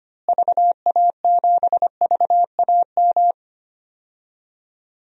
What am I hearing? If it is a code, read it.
Morse code, heard as a keyed tone, about 25 wpm, 700 Hz. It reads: VA7VAM
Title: DXCC-100 Call Signs - 6 Characters - Repeated 2x - 25wpm